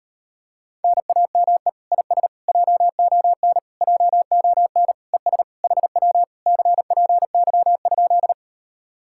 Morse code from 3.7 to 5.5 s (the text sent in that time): JON ES